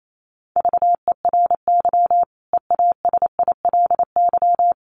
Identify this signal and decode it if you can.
Morse code, heard as a keyed tone, about 28 wpm, 700 Hz. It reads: VERY EASILY
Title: Sentences from Top 900 Words 28wpm